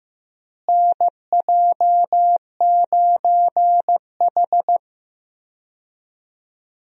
Morse code, heard as a keyed tone, about 15 wpm, 700 Hz. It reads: NJ9H
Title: USA Call Signs - 4 Characters - Repeated 3x - 15wpm